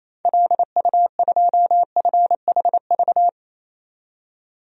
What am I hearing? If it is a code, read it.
Morse code, heard as a keyed tone, about 28 wpm, 700 Hz. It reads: LU2FHV
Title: DXCC-100 Call Signs - 6 Characters - 28wpm